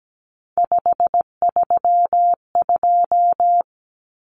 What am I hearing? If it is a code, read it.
Morse code, heard as a keyed tone, about 17 wpm, 700 Hz. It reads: 532
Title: Common Cut Numbers - 3-Digit 17wpm